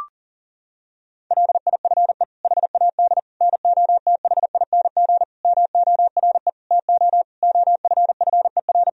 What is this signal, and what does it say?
Morse code, heard as a keyed tone, about 40 wpm, 700 Hz. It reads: LIFE HAD NOTHING MORE TO OFFER
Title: Sentences from Top 1000 Words - Review Part 4 - 40wpm